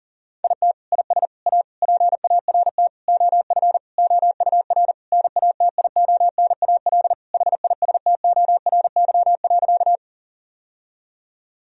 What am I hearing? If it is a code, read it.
Morse code, heard as a keyed tone, about 40 wpm, 700 Hz. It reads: IT IS A PART OF OUR NATIONAL HISTORY.
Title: Sentences from Top 1000 Words - Review Part 4 - 40wpm